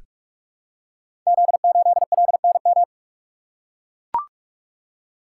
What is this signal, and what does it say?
Morse code, heard as a keyed tone, about 44 wpm, 700 Hz. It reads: Z8LNM
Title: Code Groups - 5 Letter:Numbers - 45wpm